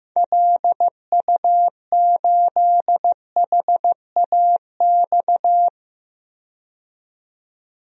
Morse code, heard as a keyed tone, about 15 wpm, 700 Hz. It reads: LU8HAX
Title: DXCC-100 Call Signs - 6 Characters - 15wpm